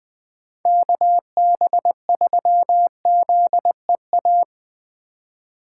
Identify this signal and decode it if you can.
Morse code, heard as a keyed tone, about 20 wpm, 700 Hz. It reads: KB3ZEA
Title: USA Call Signs - 6 Characters - 3x Character Spacing - 20wpm